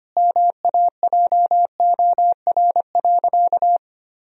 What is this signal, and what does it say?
Morse code, heard as a keyed tone, about 25 wpm, 700 Hz. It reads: MAJOR.
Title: Sentences from Top 1000 Words - Review Part 4 - 25wpm